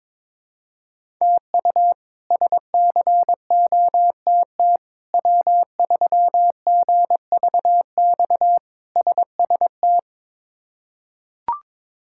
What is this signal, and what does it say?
Morse code, heard as a keyed tone, about 21 wpm, 700 Hz. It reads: TU SCOTT W3GVX SST
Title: K1USN SST QSO Practice - 22wpm